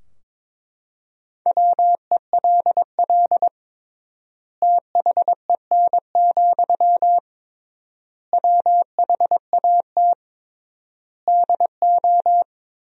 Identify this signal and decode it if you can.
Morse code, heard as a keyed tone, about 22 wpm, 700 Hz. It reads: WELL THEN, WHAT DO
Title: 3x Word-Spacing - Sentences from Top 200 Words - 22wpm